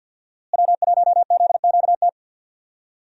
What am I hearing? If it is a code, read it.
Morse code, heard as a keyed tone, about 47 wpm, 700 Hz. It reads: W1ZQT
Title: USA Call Signs - 5 Characters - Repeated 2x - 50wpm